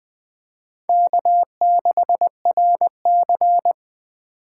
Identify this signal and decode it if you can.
Morse code, heard as a keyed tone, about 20 wpm, 700 Hz. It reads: K6RC
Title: K1USN SST QSO Practice - 20wpm